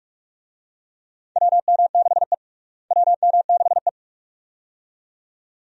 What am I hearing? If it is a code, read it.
Morse code, heard as a keyed tone, about 46 wpm, 700 Hz. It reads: WM6E WM6E
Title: USA Call Signs - 4 Characters - Repeated 2x - 45wpm